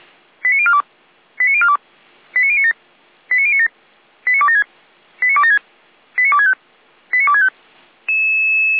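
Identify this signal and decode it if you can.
Five-tone selective call (ZVEI), heard as quick runs of five tones, tones of about 70 ms. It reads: 89042 89042 89097 89097 89276 89276 89265 89265
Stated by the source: ZVEI five-tone selective call, address 89277, received on an SDR and logged by Feuerwehr Orlamünde (f85360).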